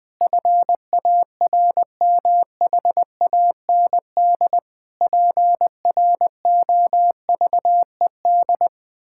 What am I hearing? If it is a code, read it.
Morse code, heard as a keyed tone, about 20 wpm, 700 Hz. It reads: FARMHAND PROVED
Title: Pangram Challenge v2 - 20wpm